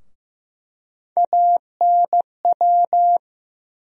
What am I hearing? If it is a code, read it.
Morse code, heard as a keyed tone, about 15 wpm, 700 Hz. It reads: ANW
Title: USA Call Sign Postfixes - 3 Letters 15wpm